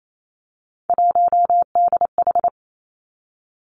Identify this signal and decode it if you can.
Morse code, heard as a keyed tone, about 28 wpm, 700 Hz. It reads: 1DH
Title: Three Challenging Letter-Number Combinations 28wpm